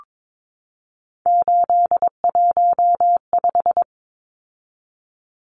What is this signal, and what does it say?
Morse code, heard as a keyed tone, about 22 wpm, 700 Hz. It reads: 815
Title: Numbers - 3-Digit 22wpm